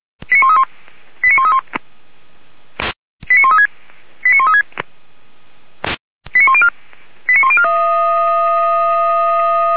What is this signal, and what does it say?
Five-tone selective call (ZVEI), heard as quick runs of five tones, tones of about 70 ms. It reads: 89131 89131 89136 89136 89104 89104